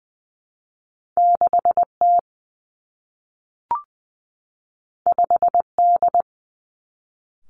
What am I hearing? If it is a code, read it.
Morse code, heard as a keyed tone, about 20 wpm, 700 Hz. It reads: 6 T 5 D
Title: Two Challenging Letter-Number Combinations 20wpm